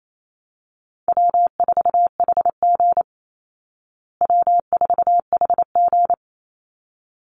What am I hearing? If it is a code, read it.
Morse code, heard as a keyed tone, about 28 wpm, 700 Hz. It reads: W4HG W4HG
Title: Call Signs Repeated 3x - 28wpm